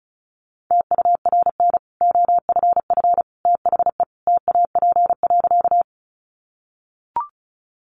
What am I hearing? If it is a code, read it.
Morse code, heard as a keyed tone, about 34 wpm, 700 Hz. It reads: TURN OFF THE TAP.
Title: Sentences from Top 1400 Words 35wpm